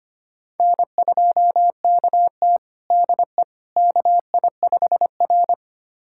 Morse code, heard as a keyed tone, about 25 wpm, 700 Hz. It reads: N2KT DE KI5R
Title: USA Callsign DE Callsign - 4 Characters - 25wpm